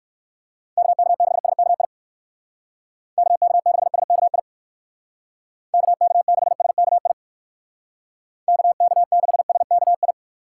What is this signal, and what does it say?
Morse code, heard as a keyed tone, about 59 wpm, 700 Hz. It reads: KK6SKI KK6SKI KK6SKI KK6SKI